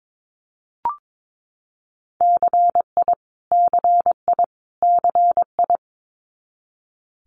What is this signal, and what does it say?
Morse code, heard as a keyed tone, about 22 wpm, 700 Hz. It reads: CI CI CI
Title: Multi-letter Phonemes Repeated 3x 22wpm